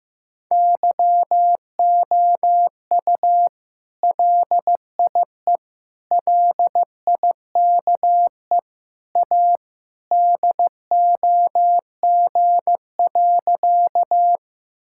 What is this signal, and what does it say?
Morse code, heard as a keyed tone, about 15 wpm, 700 Hz. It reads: YOU LIE LIKE A DOG.
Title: Sentences from Top 1600 Words 15wpm